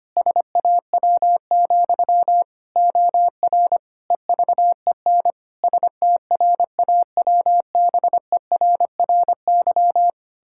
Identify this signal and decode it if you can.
Morse code, heard as a keyed tone, about 25 wpm, 700 Hz. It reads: SAW, OR EVEN STRAWBERRY
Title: Pangram Challenge v2 - 25wpm